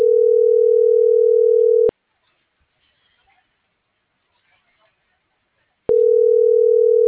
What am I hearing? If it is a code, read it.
Ringback tone (North America), 440 + 480 Hz.
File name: CENTRAL MULTIMODAL / out rodolfo.fuentes